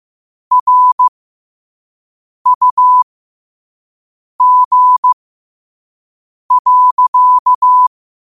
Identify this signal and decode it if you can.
Morse code, heard as a keyed tone, about 15 wpm, spaced as for 5 wpm, 1000 Hz. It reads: RUG.
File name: qotdfiles / 5wpm